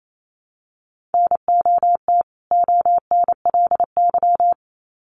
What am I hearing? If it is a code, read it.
Morse code, heard as a keyed tone, about 28 wpm, 700 Hz. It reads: NOT ONLY
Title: Sets of 2 Words 28wpm